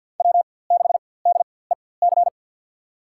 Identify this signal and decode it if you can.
Morse code, heard as a keyed tone, about 51 wpm, spaced as for 23 wpm, 700 Hz. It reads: W6DEC
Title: USA Call Signs - 5 Characters - 4x Character Spacing - 50wpm